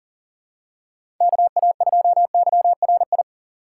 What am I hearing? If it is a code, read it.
Morse code, heard as a keyed tone, about 40 wpm, 700 Hz. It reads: KA2YRI
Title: USA Call Signs - 4x Character Spacing - 40wpm